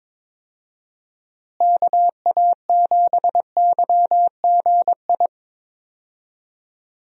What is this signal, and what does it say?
Morse code, heard as a keyed tone, about 22 wpm, 700 Hz. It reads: KA7YGI
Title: USA Call Signs - 6 Characters - Repeated 3x - 22wpm